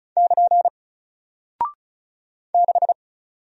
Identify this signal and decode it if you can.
Morse code, heard as a keyed tone, about 36 wpm, 700 Hz. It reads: <KN> 6
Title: Single Character (Letter:Number:Punctuation:Prosign) Alt 35wpm